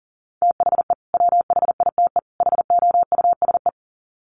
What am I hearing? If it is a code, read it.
Morse code, heard as a keyed tone, about 40 wpm, 700 Hz. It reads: THE WHITE HOUSE
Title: Sets of 3 Words 40wpm